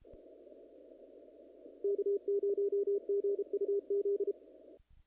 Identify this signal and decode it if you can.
Morse code, heard as a keyed tone, about 32 wpm, 400 Hz. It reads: K0GUZ
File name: contests / naqp-cw-k5tr-op-k5pi- -1900